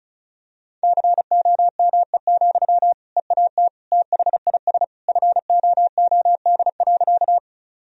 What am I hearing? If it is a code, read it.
Morse code, heard as a keyed tone, about 34 wpm, 700 Hz. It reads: COME, EAT THIS FOOD.
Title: Sentences from Top 1000 Words - Review Part 1 - 35wpm